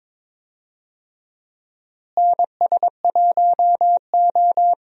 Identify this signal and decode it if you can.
Morse code, heard as a keyed tone, about 22 wpm, 700 Hz. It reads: NS1O
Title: USA Call Signs - 4 Characters - Repeated 3x - 22wpm